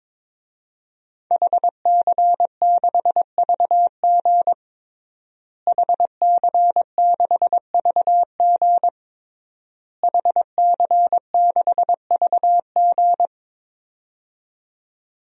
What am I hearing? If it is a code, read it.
Morse code, heard as a keyed tone, about 22 wpm, 700 Hz. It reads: HC6VG HC6VG HC6VG
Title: DXCC-100 Call Signs - 5 Characters - Repeated 3x - 22wpm